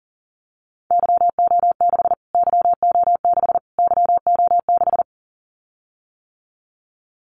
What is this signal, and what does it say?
Morse code, heard as a keyed tone, about 40 wpm, 700 Hz. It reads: YO6 YO6 YO6
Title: Callsign Prefixes for Top 25 DXCC - Repeated 3x - 40wpm